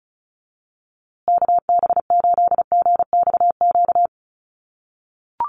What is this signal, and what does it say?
Morse code, heard as a keyed tone, about 34 wpm, 700 Hz. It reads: KB8GXQ
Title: USA Call Signs Repeated 2x 35wpm